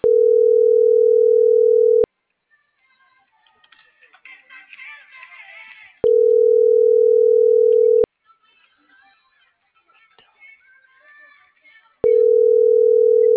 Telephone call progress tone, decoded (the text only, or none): ringback tone